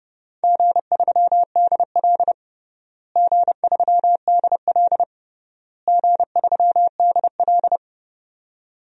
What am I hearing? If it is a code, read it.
Morse code, heard as a keyed tone, about 32 wpm, 700 Hz. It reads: G3DL G3DL G3DL
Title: DXCC-100 Call Signs - 4 Characters - Repeated 3x - 30wpm